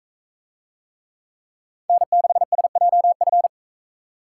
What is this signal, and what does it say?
Morse code, heard as a keyed tone, about 42 wpm, 700 Hz. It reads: N6SJF